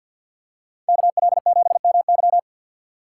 Morse code, heard as a keyed tone, about 47 wpm, 700 Hz. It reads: KL7MY
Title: USA Call Signs - 6x Character Spacing - 50wpm